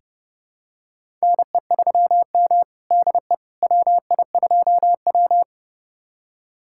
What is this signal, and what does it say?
Morse code, heard as a keyed tone, about 30 wpm, 700 Hz. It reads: NE3M DE WI2W
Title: USA Callsign DE Callsign - 4 Characters - 30wpm